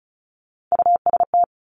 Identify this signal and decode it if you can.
Morse code, heard as a keyed tone, about 36 wpm, 700 Hz. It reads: UST
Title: Top Three Letter Combinations in Words 35wpm